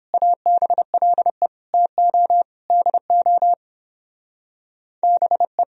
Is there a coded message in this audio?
Yes: Morse code from 0.1 to 5.6 s, heard as a keyed tone, about 31 wpm, 700 Hz. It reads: ABLE TO DO BE